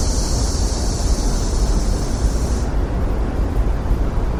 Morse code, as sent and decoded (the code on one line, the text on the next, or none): none